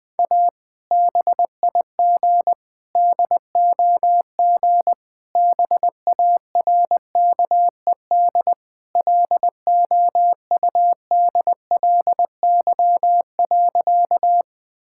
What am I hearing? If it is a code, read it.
Morse code, heard as a keyed tone, about 20 wpm, 700 Hz. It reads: A BIG DOG BARKED LOUDLY.
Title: Sentences with 2-Word Phrases - 20wpm